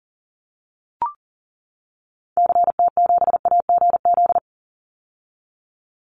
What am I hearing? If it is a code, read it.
Morse code, heard as a keyed tone, about 40 wpm, 700 Hz. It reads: CT7AGZ
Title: Call Signs - 40wpm